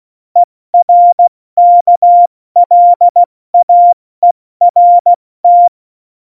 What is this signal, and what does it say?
Morse code, heard as a keyed tone, about 16 wpm, 700 Hz. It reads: ERKLAERT
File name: Morsekurs / Witze / Witz 17. November 2020, 70 F80